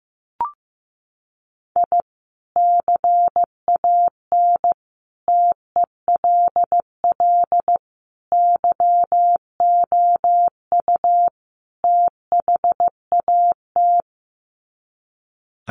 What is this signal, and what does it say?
Morse code, heard as a keyed tone, about 15 wpm, 700 Hz. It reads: I CAN TELL YOU THAT
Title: Sets of 5 Words 15wpm